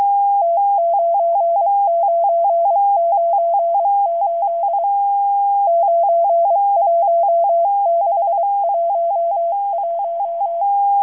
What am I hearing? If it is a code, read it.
Morse code, heard as a keyed tone, about 23 wpm, 680 Hz. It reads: T9998 91611
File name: HM02- -0437utc-7351khz